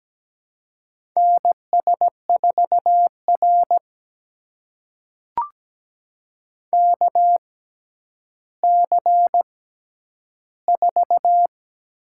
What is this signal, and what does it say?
Morse code, heard as a keyed tone, about 17 wpm, spaced as for 10 wpm, 700 Hz. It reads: NS4R KC4
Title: USA Call Signs - 6x Character Spacing - 17wpm